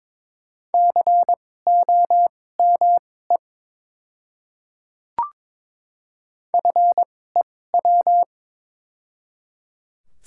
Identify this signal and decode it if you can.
Morse code, heard as a keyed tone, about 22 wpm, spaced as for 16 wpm, 700 Hz. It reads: COME FEW
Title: 2x Character Spacing - Top 300 Words - 22wpm